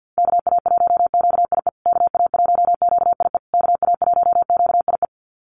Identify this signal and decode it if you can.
Morse code, heard as a keyed tone, about 48 wpm, 700 Hz. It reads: KA1QIE KA1QIE KA1QIE
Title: USA Call Signs Repeated 3x 50wpm